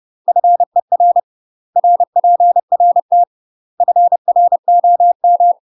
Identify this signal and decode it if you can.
Morse code, heard as a keyed tone, about 30 wpm, 700 Hz. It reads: FER RPRT FROM
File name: UK MASTER QSOs TRAINING / 30WPM / UK MASTER QSOs 0042